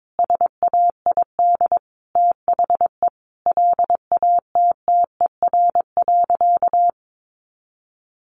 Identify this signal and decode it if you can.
Morse code, heard as a keyed tone, about 22 wpm, 700 Hz. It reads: SAID THE LATTER.
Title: A Sound Of Thunder 22wpm